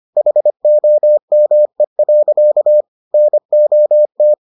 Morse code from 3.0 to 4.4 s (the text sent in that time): NOT